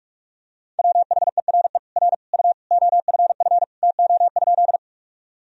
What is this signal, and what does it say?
Morse code, heard as a keyed tone, about 44 wpm, 700 Hz. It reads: WHERE R U OFF TO?
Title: Sentences from Top 200 Words and Abbreviations 45wpm